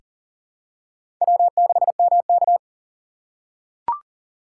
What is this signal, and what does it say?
Morse code, heard as a keyed tone, about 40 wpm, 700 Hz. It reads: W6MK
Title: USA Call Signs - 4 Characters - 3x Character Spacing - 40wpm